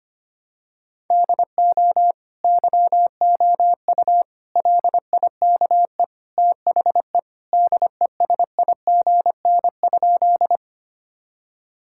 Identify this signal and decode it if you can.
Morse code, heard as a keyed tone, about 25 wpm, 700 Hz. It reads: DO YOU LIKE THE DESIGN?
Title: Sentences from Top 1000 Words - Review Part 5 - 25wpm